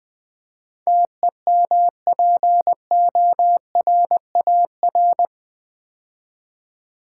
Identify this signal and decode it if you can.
Morse code, heard as a keyed tone, about 20 wpm, 700 Hz. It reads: TEMPORAR
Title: Build A Word - 4 Letters At A Time - 11-Letter Words - 20wpm